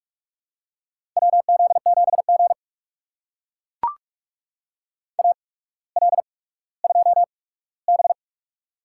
Morse code, heard as a keyed tone, about 44 wpm, spaced as for 19 wpm, 700 Hz. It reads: WZ7G AL2B
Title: USA Call Signs - 4 Characters - 8x Character Spacing - 45wpm